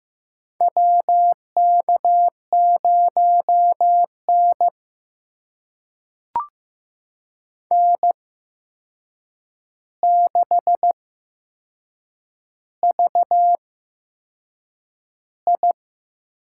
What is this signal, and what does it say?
Morse code, heard as a keyed tone, about 15 wpm, spaced as for 6 wpm, 700 Hz. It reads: WK0N N6VI
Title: USA Call Signs - 8x Character Spacing - 15wpm